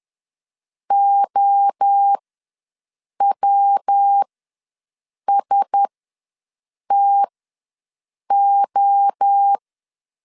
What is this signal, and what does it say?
Morse code, heard as a keyed tone, about 10 wpm, spaced as for 6 wpm, 800 Hz. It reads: OWSTO
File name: LEZIONI / 5 LBVC / 5 LB ws 10wpm 3616